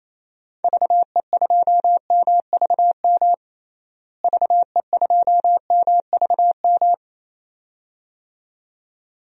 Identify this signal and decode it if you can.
Morse code, heard as a keyed tone, about 28 wpm, 700 Hz. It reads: VE2MVM VE2MVM
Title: DXCC-100 Call Signs - 6 Characters - Repeated 2x - 28wpm